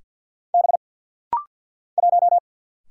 Morse code, heard as a keyed tone, about 50 wpm, 700 Hz. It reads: B 1
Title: Single Letter-Number - Rapid-Fire 50wpm